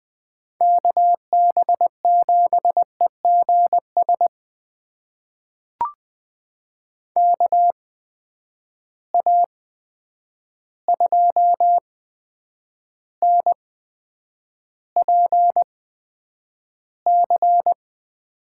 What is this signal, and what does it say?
Morse code, heard as a keyed tone, about 20 wpm, spaced as for 9 wpm, 700 Hz. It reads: KB7EGS KA2NPC
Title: USA Call Signs - 6 Characters - 8x Character Spacing - 20wpm